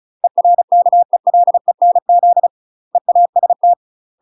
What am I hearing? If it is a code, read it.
Morse code, heard as a keyed tone, about 36 wpm, 700 Hz. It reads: ERKELENZ EAST
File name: UK MASTER QSOs TRAINING / 35WPM / UK MASTER QSOs 0035